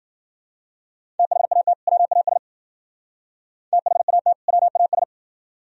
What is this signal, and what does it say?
Morse code, heard as a keyed tone, about 52 wpm, 700 Hz. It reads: THAT WAS THAT WAS